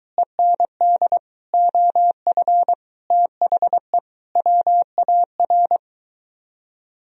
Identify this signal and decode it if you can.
Morse code, heard as a keyed tone, about 23 wpm, 700 Hz. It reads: END OF THE WAR